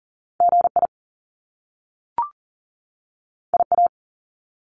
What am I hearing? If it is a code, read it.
Morse code, heard as a keyed tone, about 40 wpm, 700 Hz. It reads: GI IA
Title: Top Two Letter Combinations in Words 40wpm